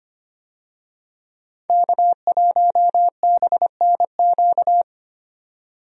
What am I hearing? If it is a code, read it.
Morse code, heard as a keyed tone, about 25 wpm, 700 Hz. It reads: K1BNQ
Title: USA Call Signs - 5 Characters - 25wpm